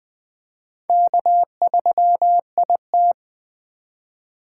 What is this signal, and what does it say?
Morse code, heard as a keyed tone, about 20 wpm, 700 Hz. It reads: K3IT
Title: Speed Racing - USA 1x2 Call Signs 17wpm